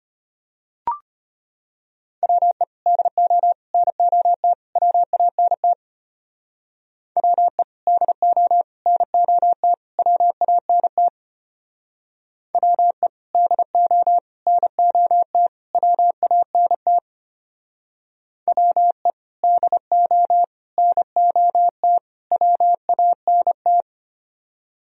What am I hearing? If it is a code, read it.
Morse code, heard as a keyed tone, about 30 wpm, 700 Hz. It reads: WE DO NOT WANT WE DO NOT WANT WE DO NOT WANT WE DO NOT WANT
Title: Speed-Racing - Sets of 4 Words 25wpm